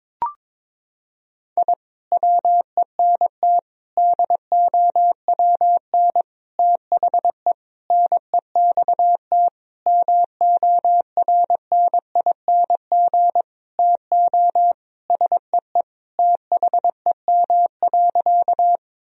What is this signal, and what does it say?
Morse code, heard as a keyed tone, about 22 wpm, 700 Hz. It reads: I WENT DOWN THE NEXT MORNING TO SEE THEM.
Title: Sentences from Top 1000 Words - Review Part 1 - 22wpm